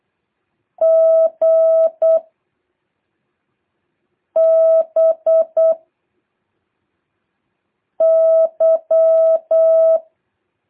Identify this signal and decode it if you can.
Morse code, heard as a keyed tone, about 8 wpm, 660 Hz. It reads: G B Y
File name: jh8cbh / 2023cw